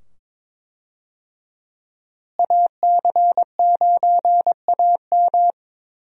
Morse code, heard as a keyed tone, about 22 wpm, 700 Hz. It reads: AC9AM
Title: USA Call Signs - 5 Characters - Repeated 2x - 22wpm